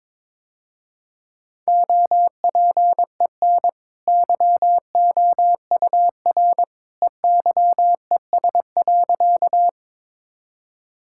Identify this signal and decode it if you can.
Morse code, heard as a keyed tone, about 22 wpm, 700 Hz. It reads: OPEN YOUR EYES.
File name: practice / 22wpm / Easy Sentences with Hints - 22wpm